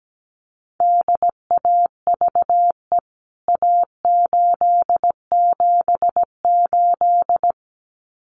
Morse code, heard as a keyed tone, about 17 wpm, 700 Hz. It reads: DAVE A878
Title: CWT Exchange 17wpm